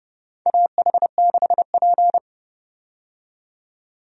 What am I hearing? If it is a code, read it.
Morse code, heard as a keyed tone, about 30 wpm, 700 Hz. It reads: AH6P
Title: USA Call Signs - 4 Characters - 30wpm